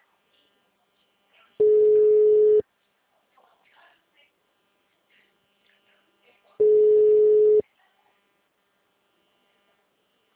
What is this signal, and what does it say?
Ringback tone (Europe, 425 Hz), with speech.